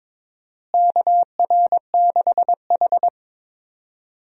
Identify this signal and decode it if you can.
Morse code, heard as a keyed tone, about 22 wpm, 700 Hz. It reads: KR6H
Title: USA Call Signs - 4 Characters - Repeated 3x - 22wpm